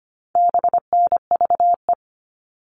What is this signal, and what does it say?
Morse code, heard as a keyed tone, about 25 wpm, 700 Hz. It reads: BNVE
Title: All Cut Numbers - 4-Digit 25wpm